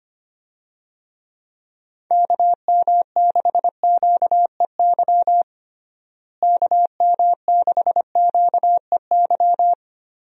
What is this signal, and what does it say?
Morse code, heard as a keyed tone, about 25 wpm, 700 Hz. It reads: KM6QEY KM6QEY
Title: USA Call Signs - 6 Characters - Repeated 2x - 25wpm